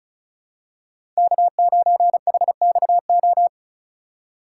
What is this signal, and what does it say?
Morse code, heard as a keyed tone, about 34 wpm, 700 Hz. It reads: K9HXO
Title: USA Call Signs - 5 Characters - 4x Character Spacing - 35wpm